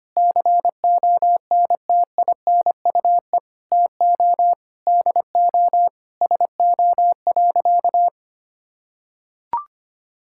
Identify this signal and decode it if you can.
Morse code, heard as a keyed tone, about 25 wpm, 700 Hz. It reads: CONTINUE TO DO SO.
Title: Sentences from Top 1000 Words - Review Part 6 - 25wpm